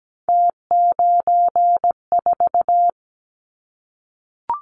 Morse code, heard as a keyed tone, about 17 wpm, 700 Hz. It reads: T94
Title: Callsign Prefixes for Top 100 DXCC - 17wpm